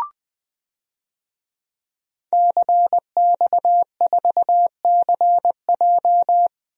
Morse code, heard as a keyed tone, about 20 wpm, 700 Hz. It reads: CX4CJ
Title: DXCC-100 Call Signs - 5 Characters - Repeated 3x - 20wpm